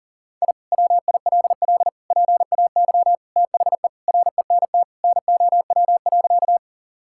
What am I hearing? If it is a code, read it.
Morse code, heard as a keyed tone, about 40 wpm, 700 Hz. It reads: I WILL PAY THE RENT NOW.
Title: Sentences from Top 1000 Words - Review Part 5 - 40wpm